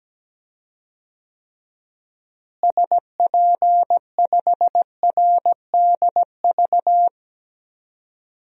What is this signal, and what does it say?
Morse code, heard as a keyed tone, about 17 wpm, 700 Hz. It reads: SP5RDV
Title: DXCC-100 Call Signs - 6 Characters - Repeated 3x - 17wpm